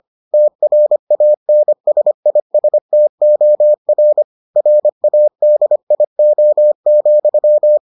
Morse code, heard as a keyed tone, about 25 wpm, 600 Hz. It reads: TRANSISTOR RADIO,